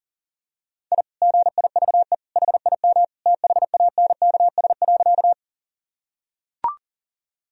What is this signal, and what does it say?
Morse code, heard as a keyed tone, about 37 wpm, 700 Hz. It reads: I GIVE HIM THANKS.
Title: Sentences from Top 1000 Words - Review Part 3 - 40wpm